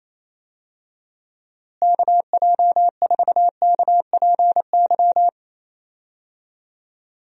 Morse code, heard as a keyed tone, about 28 wpm, 700 Hz. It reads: KJ4KPY
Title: USA Call Signs - 6 Characters - Repeated 2x - 28wpm